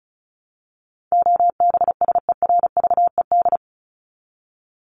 Morse code, heard as a keyed tone, about 34 wpm, 700 Hz. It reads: OBSERVED
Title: ED Suffix - Top 100 Words with Base Words 35wpm